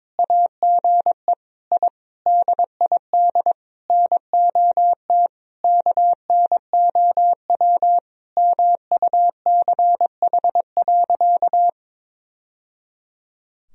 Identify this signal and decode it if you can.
Morse code, heard as a keyed tone, about 22 wpm, 700 Hz. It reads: AGE I DID NOT KNOW MUCH.
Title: Sentences from Top 1000 Words - Review Part 3 - 22wpm